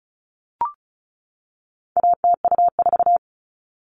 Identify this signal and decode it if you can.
Morse code, heard as a keyed tone, about 34 wpm, 700 Hz. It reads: ATU4
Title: All Cut Numbers - 4-Digit 35wpm